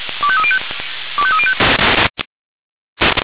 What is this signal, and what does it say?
Five-tone selective call (ZVEI), heard as quick runs of five tones, tones of about 70 ms. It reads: 25305 25305
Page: ZVEI